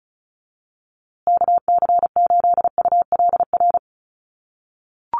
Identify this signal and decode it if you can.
Morse code, heard as a keyed tone, about 34 wpm, 700 Hz. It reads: KC8ULR
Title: USA Call Signs Repeated 2x 35wpm